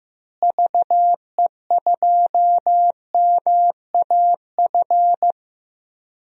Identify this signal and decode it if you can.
Morse code, heard as a keyed tone, about 15 wpm, 700 Hz. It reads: VE2MAF
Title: DXCC-100 Call Signs - 6 Characters - Repeated 3x - 15wpm